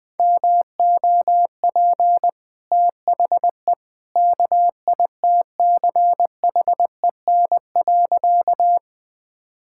Morse code, heard as a keyed tone, about 20 wpm, 700 Hz. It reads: MOP THE KITCHEN.
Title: Easy Sentences with Hints - 20wpm